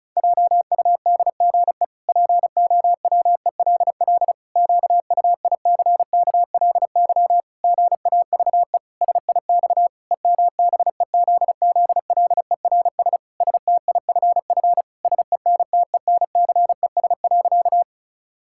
Morse code, heard as a keyed tone, about 36 wpm, 700 Hz. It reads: JUDGE POWELL QUICKLY GAVE SIX EMBEZZLERS STIFF SENTENCES.
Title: Pangram Challenge v2 - 35wpm